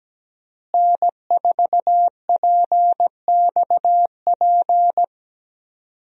Morse code, heard as a keyed tone, about 17 wpm, 700 Hz. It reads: N4PXP
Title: USA Call Signs - 5 Characters - Repeated 3x - 17wpm